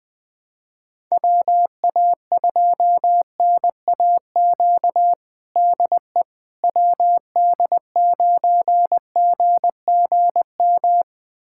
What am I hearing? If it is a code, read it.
Morse code, heard as a keyed tone, about 21 wpm, 700 Hz. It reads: WA2NAQ DE WD9GGM
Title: USA Callsign DE Callsign - 6 Characters - 20wpm